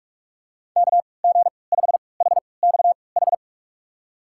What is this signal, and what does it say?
Morse code, heard as a keyed tone, about 42 wpm, spaced as for 25 wpm, 690 Hz. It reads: KG5HXH
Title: USA Call Signs - 6 Characters - 3x Character Spacing - 45wpm